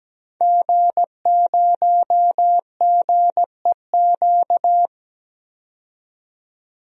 Morse code, heard as a keyed tone, about 17 wpm, 700 Hz. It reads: G0GEQ
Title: DXCC-100 Call Signs - 5 Characters - 17wpm